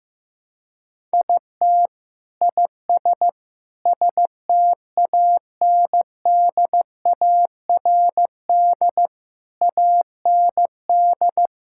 Morse code, heard as a keyed tone, about 15 wpm, 700 Hz. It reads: IT IS STANDARD AND
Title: Sentences from Top 1000 Words - Review Part 6 - 15wpm